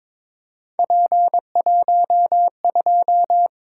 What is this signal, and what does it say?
Morse code, heard as a keyed tone, about 22 wpm, 700 Hz. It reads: P12
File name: practice / 22wpm / Three Letter-Number Combinations - 22wpm